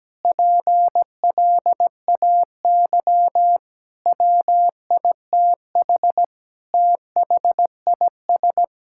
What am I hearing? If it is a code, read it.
Morse code, heard as a keyed tone, about 17 wpm, 700 Hz. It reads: PLAY WITH THIS
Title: Sentences from Top 1000 Words - Review Part 6 - 17wpm